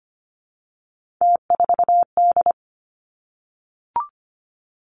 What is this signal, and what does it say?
Morse code, heard as a keyed tone, about 25 wpm, 700 Hz. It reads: T4D
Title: Three Challenging Letter-Number Combinations 25wpm